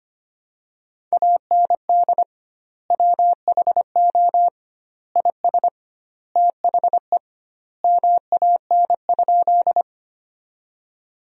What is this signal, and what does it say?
Morse code, heard as a keyed tone, about 25 wpm, 700 Hz. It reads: AND WHO IS THE MAN?
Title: 2x Word-Spacing - Sentences from Top 200 Words - 25wpm